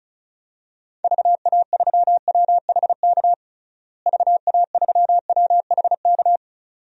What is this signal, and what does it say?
Morse code, heard as a keyed tone, about 36 wpm, 700 Hz. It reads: VA3WHK VA3WHK
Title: DXCC-100 Call Signs - 6 Characters - Repeated 2x - 35wpm